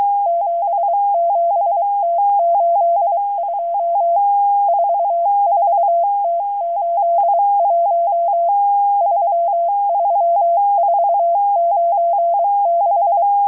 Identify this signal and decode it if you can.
Morse code, heard as a keyed tone, about 23 wpm, 680 Hz. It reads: 77T82 44T81 33496